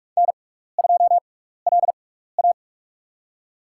Morse code, heard as a keyed tone, about 46 wpm, spaced as for 15 wpm, 700 Hz. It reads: N2LA
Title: USA Call Signs - 4 Characters - 6x Character Spacing - 45wpm